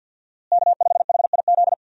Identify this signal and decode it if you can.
Morse code, heard as a keyed tone, about 53 wpm, 700 Hz. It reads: K5HIZ
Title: USA Call Signs - 5 Characters - Repeated 2x - 50wpm